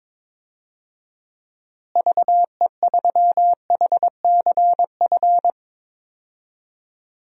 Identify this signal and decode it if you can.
Morse code, heard as a keyed tone, about 22 wpm, 700 Hz. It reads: VE3HCF
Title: DXCC-100 Call Signs - 6 Characters - Repeated 3x - 22wpm